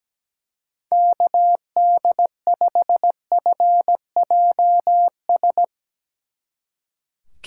Morse code, heard as a keyed tone, about 19 wpm, 700 Hz. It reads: KD5FJS
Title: Speed Racing - USA 2x3 Call Signs 17wpm